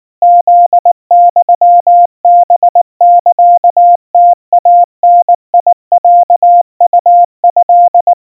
Morse code (Japanese Wabun code), heard as a keyed tone, about 19 wpm, 700 Hz. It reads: フユハサムイダロウト
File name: kawa / Morse sample / kamyu 70 cpm 700 Hz Crispy